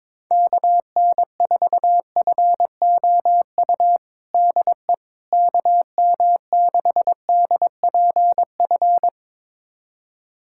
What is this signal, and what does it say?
Morse code, heard as a keyed tone, about 22 wpm, 700 Hz. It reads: KN4FOU DE KM6DPF
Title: USA Callsign DE Callsign - 6 Characters - 22wpm